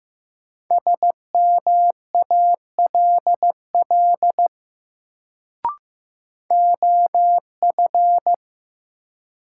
Morse code, heard as a keyed tone, about 15 wpm, 700 Hz. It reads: SMALL OF
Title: Sets of 2 Words - New Word Drill - 15wpm